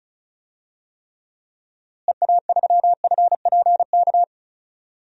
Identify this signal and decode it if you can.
Morse code, heard as a keyed tone, about 36 wpm, 700 Hz. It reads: EA3FPK
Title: DXCC-100 Call Signs - 6 Characters - Repeated 2x - 35wpm